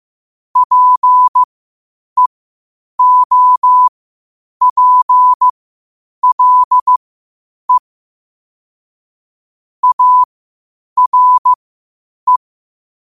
Morse code, heard as a keyed tone, about 15 wpm, spaced as for 8 wpm, 1000 Hz. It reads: PEOPLE ARE
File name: qotdfiles / 8wpm